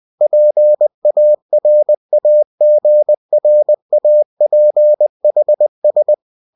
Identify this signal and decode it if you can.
Morse code, heard as a keyed tone, about 20 wpm, 600 Hz. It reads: PARAGRAPHS
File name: cw-books / winnie-the-pooh / winnie-20wpm-page-0115